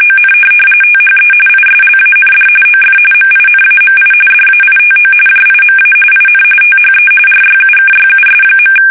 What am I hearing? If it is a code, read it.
RTTY, about 45.45 baud. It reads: ON THE ISLAND OF MINAMI DAITO JIMA IN THE RYUKYUS.
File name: fsk850 / USS-Massachusetts